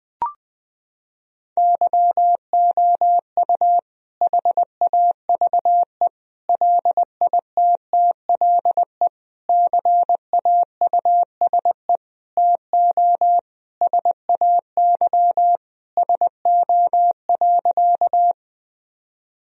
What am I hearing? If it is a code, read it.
Morse code, heard as a keyed tone, about 20 wpm, 700 Hz. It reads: YOU HAVE LITTLE CAUSE TO SAY SO.
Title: Sentences from Top 1000 Words - Review Part 1 - 20wpm